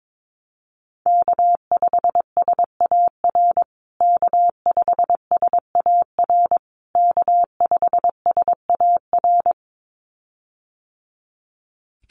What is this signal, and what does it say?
Morse code, heard as a keyed tone, about 22 wpm, 700 Hz. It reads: K5SAR K5SAR K5SAR
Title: USA Call Signs Repeated 3x 22wpm